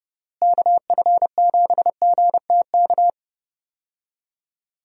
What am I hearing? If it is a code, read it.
Morse code, heard as a keyed tone, about 30 wpm, 700 Hz. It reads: KF7GTK
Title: USA Call Signs - 6 Characters - Repeated 2x - 30wpm